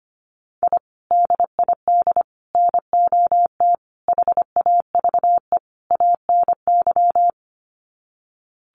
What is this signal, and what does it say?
Morse code, heard as a keyed tone, about 25 wpm, 700 Hz. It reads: I DID NOT HAVE ANY
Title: Sets of 5 Words Encore 25wpm